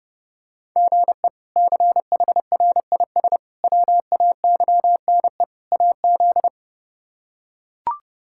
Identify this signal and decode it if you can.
Morse code, heard as a keyed tone, about 32 wpm, 700 Hz. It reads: GE CHRIS WAYNE AZ
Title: K1USN SST QSO Practice - 30wpm